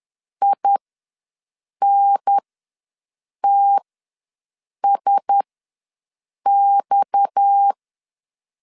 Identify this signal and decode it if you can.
Morse code, heard as a keyed tone, about 10 wpm, spaced as for 6 wpm, 800 Hz. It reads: INTSX
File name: LEZIONI / 6 PKX0 / 6 PKX0 ws 10wpm 4220